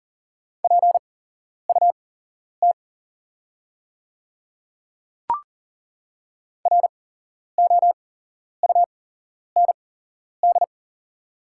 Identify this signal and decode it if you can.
Morse code, heard as a keyed tone, about 40 wpm, spaced as for 11 wpm, 690 Hz. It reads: PUT ROUND
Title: 8x Character Spacing - Top 200 Words - 40wpm